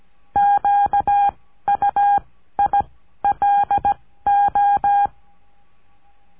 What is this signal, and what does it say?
Morse code, heard as a keyed tone, about 17 wpm, 810 Hz. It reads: Q U I L O
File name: cursos / CURSO DE TELEGRAFIA / Telegrafia - Trilha 14